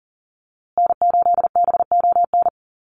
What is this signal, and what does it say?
Morse code, heard as a keyed tone, about 40 wpm, 700 Hz. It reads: N8BON
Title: USA Call Signs Repeated 2x 40wpm